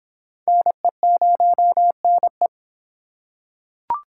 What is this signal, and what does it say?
Morse code, heard as a keyed tone, about 26 wpm, 700 Hz. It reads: NE0NE